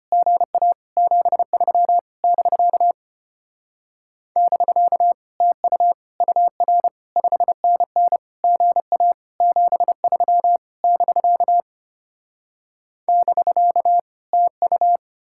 Morse code, heard as a keyed tone, about 33 wpm, 700 Hz. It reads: GA 73 <BK> <BK> TU UR 5NN GA 73 <BK> <BK> TU